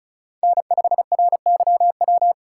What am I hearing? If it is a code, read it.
Morse code, heard as a keyed tone, about 36 wpm, 700 Hz. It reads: N5RYW
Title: USA Call Signs - 6x Character Spacing - 35wpm